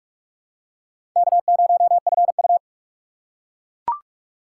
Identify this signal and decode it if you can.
Morse code, heard as a keyed tone, about 42 wpm, 700 Hz. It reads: K0FU
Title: USA Call Signs - 4 Characters - 4x Character Spacing - 45wpm